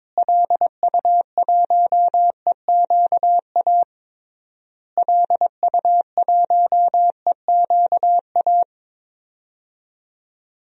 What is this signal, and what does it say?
Morse code, heard as a keyed tone, about 22 wpm, 700 Hz. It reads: LU1EQA LU1EQA
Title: DXCC-100 Call Signs - 6 Characters - Repeated 3x - 22wpm